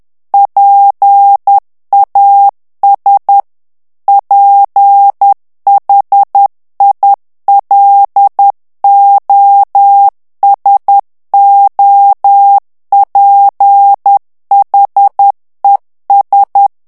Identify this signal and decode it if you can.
Morse code, heard as a keyed tone, about 10 wpm, 800 Hz. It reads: PAS PHILOSOPHES
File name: FRANCAIS TEXTE CLAIR 2 VITESSE 10 MOTS MINUTES